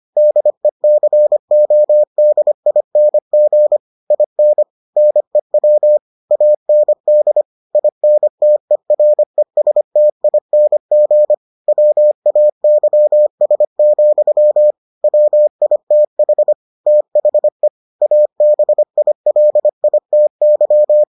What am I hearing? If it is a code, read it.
Morse code, heard as a keyed tone, about 25 wpm, 600 Hz. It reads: DECODING IN NEW AND INTERESTING WAYS, WITH THE ABILITY